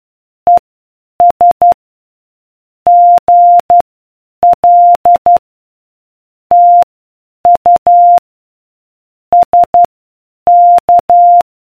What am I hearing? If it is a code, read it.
Morse code, heard as a keyed tone, about 12 wpm, spaced as for 8 wpm, 700 Hz. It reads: ES GL TU SK